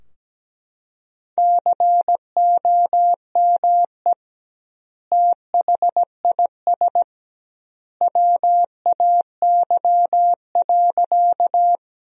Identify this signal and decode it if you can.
Morse code, heard as a keyed tone, about 17 wpm, 700 Hz. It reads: COME THIS WAY.
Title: 2x Word-Spacing - Sentences from Top 100 Words - 17wpm